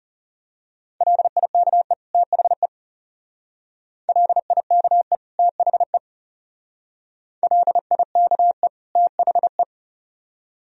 Morse code, heard as a keyed tone, about 34 wpm, 700 Hz. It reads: LIKE THE LIKE THE LIKE THE
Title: Speed-Racing - Sets of 2 Words 30wpm